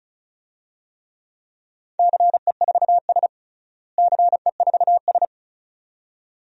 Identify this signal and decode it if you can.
Morse code, heard as a keyed tone, about 34 wpm, 700 Hz. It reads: CE4S CE4S
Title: DXCC-100 Call Signs - 4 Characters - Repeated 2x - 35wpm